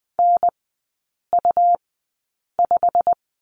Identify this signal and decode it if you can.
Morse code, heard as a keyed tone, about 20 wpm, spaced as for 8 wpm, 700 Hz. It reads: NU5
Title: 114 Lesson 19e - Letter w - Three Characters 20wpm